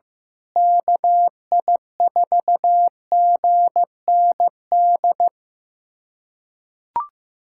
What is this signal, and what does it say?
Morse code, heard as a keyed tone, about 15 wpm, 700 Hz. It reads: KI4GND
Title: USA Callsign DE Callsign 15wpm